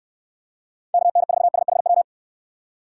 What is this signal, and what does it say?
Morse code, heard as a keyed tone, about 66 wpm, 700 Hz. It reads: KN4SHW